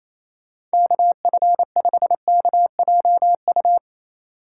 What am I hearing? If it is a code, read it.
Morse code, heard as a keyed tone, about 34 wpm, 700 Hz. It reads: KF5KJU
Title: Speed Racing - USA 2x3 Call Signs 28wpm